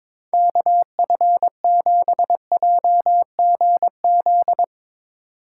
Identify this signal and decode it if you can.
Morse code, heard as a keyed tone, about 22 wpm, 700 Hz. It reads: KF7JGZ
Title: Speed Racing - USA 2x3 Call Signs 22wpm